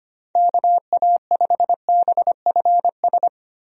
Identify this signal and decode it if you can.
Morse code, heard as a keyed tone, about 25 wpm, 700 Hz. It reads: KA5BFS
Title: USA Call Signs - 6 Characters - Repeated 2x - 25wpm